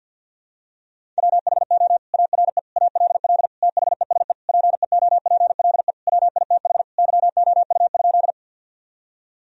Morse code, heard as a keyed tone, about 52 wpm, 700 Hz. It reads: WHO ARE ALL THESE PEOPLE WITH YOU?
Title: Sentences from Top 1000 Words - Review Part 1 - 50wpm